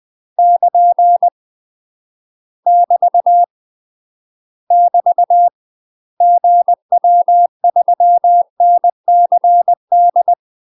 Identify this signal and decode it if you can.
Morse code, heard as a keyed tone, about 20 wpm, 700 Hz. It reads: <KN> <BT> <BT> GW3NCD
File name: 20WPM